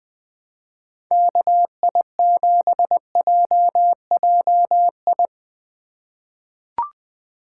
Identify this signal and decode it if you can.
Morse code, heard as a keyed tone, about 20 wpm, 700 Hz. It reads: KI7JJI
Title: USA Call Signs - 6 Characters - 2x Character Spacing - 20wpm